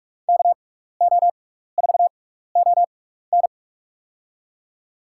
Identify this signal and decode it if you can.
Morse code, heard as a keyed tone, about 46 wpm, spaced as for 15 wpm, 700 Hz. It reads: KO4ON
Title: USA Call Signs - 6x Character Spacing - 45wpm